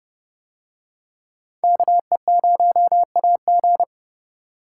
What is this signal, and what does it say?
Morse code, heard as a keyed tone, about 30 wpm, 700 Hz. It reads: KE0AG